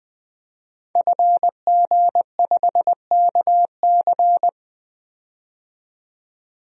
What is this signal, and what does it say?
Morse code, heard as a keyed tone, about 21 wpm, 700 Hz. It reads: FG5KC
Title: DXCC-100 Call Signs - 5 Characters - 20wpm